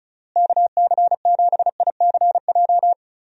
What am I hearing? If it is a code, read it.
Morse code, heard as a keyed tone, about 34 wpm, 700 Hz. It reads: KC7ICJ
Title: USA Call Signs - 6 Characters - Repeated 3x - 35wpm